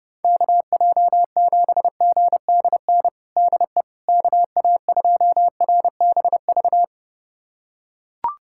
Morse code, heard as a keyed tone, about 29 wpm, 700 Hz. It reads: KJ7GDN DE KA2RBV
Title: USA Callsign DE Callsign - 6 Characters - 30wpm